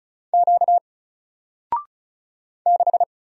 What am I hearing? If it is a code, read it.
Morse code, heard as a keyed tone, about 35 wpm, 700 Hz. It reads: Q 6
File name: Single-Letter-Number-35wpm